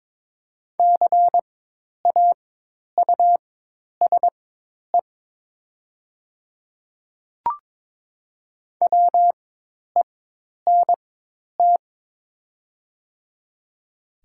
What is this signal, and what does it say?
Morse code, heard as a keyed tone, about 22 wpm, spaced as for 10 wpm, 700 Hz. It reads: CAUSE WENT
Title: 4x Character Spacing - Top 300 Words - 22wpm